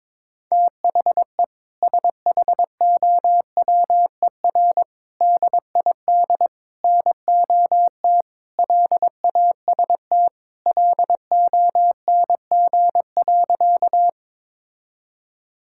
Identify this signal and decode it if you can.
Morse code, heard as a keyed tone, about 22 wpm, 700 Hz. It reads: THE SHOWER DID NOT LAST LONG.
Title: Sentences from Top 1600 Words 22wpm